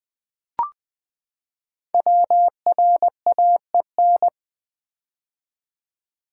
Morse code, heard as a keyed tone, about 20 wpm, 700 Hz. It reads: WRAEN
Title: Code Groups - 5 Letters - 20wpm